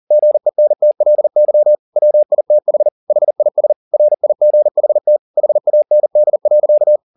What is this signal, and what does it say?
Morse code, heard as a keyed tone, about 40 wpm, 600 Hz. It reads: GENTLY WITH HIS RIGHT HAND.
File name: the old man and the sea 2 40wpm